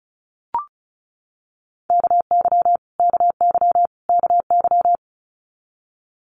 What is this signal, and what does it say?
Morse code, heard as a keyed tone, about 34 wpm, 700 Hz. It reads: KY KY KY
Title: US States Repeated 3x 35wpm